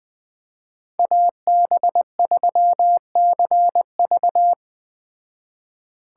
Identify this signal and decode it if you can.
Morse code, heard as a keyed tone, about 20 wpm, 700 Hz. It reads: AB3CV